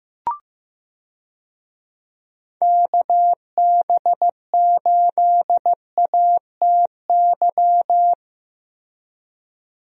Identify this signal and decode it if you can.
Morse code, heard as a keyed tone, about 15 wpm, 700 Hz. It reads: KB8ATY
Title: USA Call Signs - 6 Characters - Repeated 2x - 15wpm